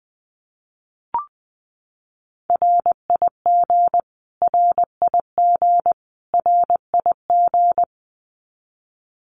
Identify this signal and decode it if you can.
Morse code, heard as a keyed tone, about 20 wpm, 700 Hz. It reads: RIG RIG RIG
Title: Abbreviations Repeated 3x - 20wpm